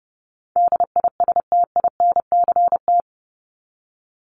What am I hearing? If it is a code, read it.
Morse code, heard as a keyed tone, about 30 wpm, 700 Hz. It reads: DISTINCT
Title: Top 1501-1750 Words 30wpm